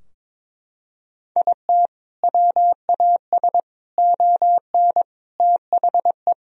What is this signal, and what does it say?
Morse code, heard as a keyed tone, about 22 wpm, 700 Hz. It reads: IT WAS ON THE
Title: Sentences from Top 1000 Words - Review Part 3 - 22wpm